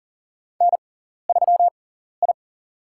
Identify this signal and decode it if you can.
Morse code, heard as a keyed tone, about 40 wpm, spaced as for 14 wpm, 700 Hz. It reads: N3I
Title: USA Call Signs - 6x Character Spacing - 40wpm